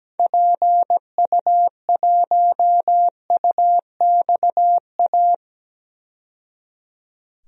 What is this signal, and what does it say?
Morse code, heard as a keyed tone, about 17 wpm, 700 Hz. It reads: PU1UXA
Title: DXCC-100 Call Signs - 6 Characters - 17wpm